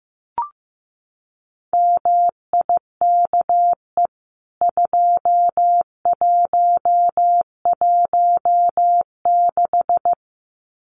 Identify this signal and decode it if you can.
Morse code, heard as a keyed tone, about 15 wpm, 700 Hz. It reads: MIKE 2116
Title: CWT Exchange 15wpm